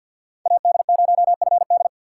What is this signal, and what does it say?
Morse code, heard as a keyed tone, about 50 wpm, 700 Hz. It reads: AD0FD
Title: USA Call Signs - 5 Characters - 2x Character Spacing - 50wpm